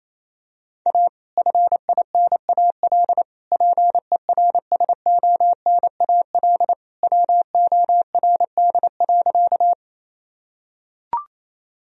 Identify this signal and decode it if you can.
Morse code, heard as a keyed tone, about 28 wpm, 700 Hz. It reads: A FINAL PERSONAL WORD.
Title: Sentences from Top 1000 Words - Review Part 4 - 28wpm